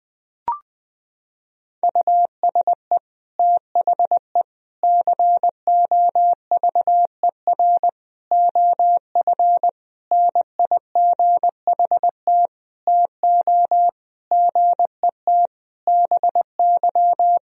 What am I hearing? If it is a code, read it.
Morse code, heard as a keyed tone, about 21 wpm, 700 Hz. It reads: USE THE COVER OF NIGHT TO GET BY
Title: Sentences from Top 1000 Words - Review Part 1 - 20wpm